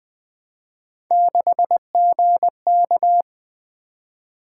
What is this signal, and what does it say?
Morse code, heard as a keyed tone, about 21 wpm, 700 Hz. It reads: 6GK
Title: Three Letter-Number Combinations - 20wpm